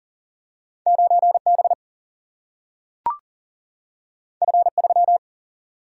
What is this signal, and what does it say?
Morse code, heard as a keyed tone, about 40 wpm, 700 Hz. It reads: 9 B F 3
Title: Single-Letter Single-Number 40wpm